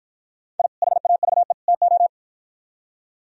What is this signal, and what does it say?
Morse code, heard as a keyed tone, about 52 wpm, 700 Hz. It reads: I HAVE TO